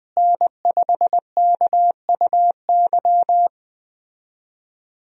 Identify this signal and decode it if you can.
Morse code, heard as a keyed tone, about 20 wpm, 700 Hz. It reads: N5KUY
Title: USA Call Signs - 5 Characters - Repeated 3x - 20wpm